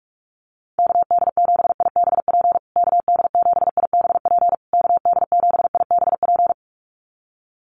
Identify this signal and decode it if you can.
Morse code, heard as a keyed tone, about 44 wpm, 700 Hz. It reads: KD7IBP KD7IBP KD7IBP
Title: USA Call Signs Repeated 3x 45wpm